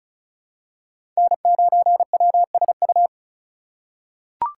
Morse code, heard as a keyed tone, about 36 wpm, 700 Hz. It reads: N9WSU
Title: USA Call Signs - 5 Characters - 2x Character Spacing - 35wpm